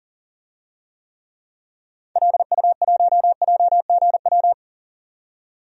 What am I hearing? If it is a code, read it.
Morse code, heard as a keyed tone, about 40 wpm, 700 Hz. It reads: LU1JGW
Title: DXCC-100 Call Signs - 6 Characters - 40wpm